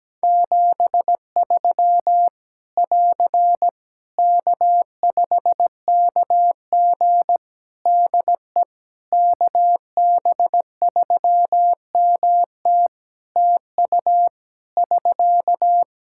Morse code, heard as a keyed tone, about 17 wpm, 700 Hz. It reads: 73 <AR> K5KG DE KB3MT TU <SK>
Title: K1USN SST QSO Practice 17wpm